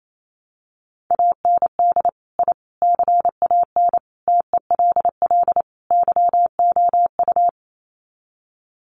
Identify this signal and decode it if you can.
Morse code, heard as a keyed tone, about 28 wpm, 700 Hz. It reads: AND I CAN TELL YOU
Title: Sets of 5 Words Encore2 28wpm